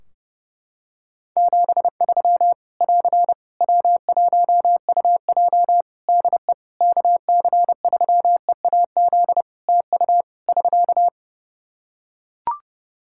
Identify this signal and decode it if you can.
Morse code, heard as a keyed tone, about 30 wpm, 700 Hz. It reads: 73 <AR> W1UJ DE KC3EAZ TU <SK>
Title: Practice QSO - Exchange 4 - Ending - 30wpm